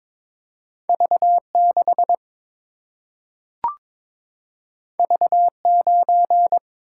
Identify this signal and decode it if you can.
Morse code, heard as a keyed tone, about 21 wpm, 700 Hz. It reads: V 6 V 9
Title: Single-Letter Single-Number 22wpm